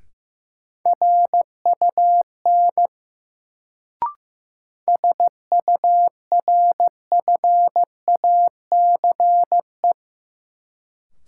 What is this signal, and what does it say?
Morse code, heard as a keyed tone, about 15 wpm, 700 Hz. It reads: RUN SURFACE
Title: Sets of 4 Words - New Word Drill - 15wpm